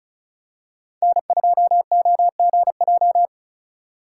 Morse code, heard as a keyed tone, about 34 wpm, 700 Hz. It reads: N2OGJ
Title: USA Call Signs - 5 Characters - 3x Character Spacing - 35wpm